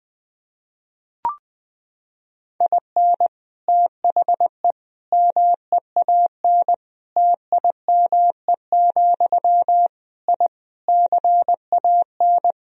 Morse code, heard as a keyed tone, about 20 wpm, 700 Hz. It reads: IN THE MEAN TIME, I CAN
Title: Sentences from Top 1000 Words - Review Part 3 - 20wpm